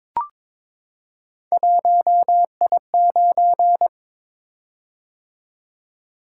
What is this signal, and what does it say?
Morse code, heard as a keyed tone, about 22 wpm, 700 Hz. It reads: 1I9
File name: Three Letter-Number Combinations - 22wpm